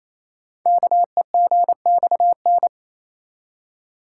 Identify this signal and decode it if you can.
Morse code, heard as a keyed tone, about 27 wpm, 700 Hz. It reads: KEGXN
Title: Code Groups - 5 Letters - 28wpm